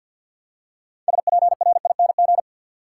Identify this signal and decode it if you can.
Morse code, heard as a keyed tone, about 53 wpm, 700 Hz. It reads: SPRING
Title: Six Letter Words - 50wpm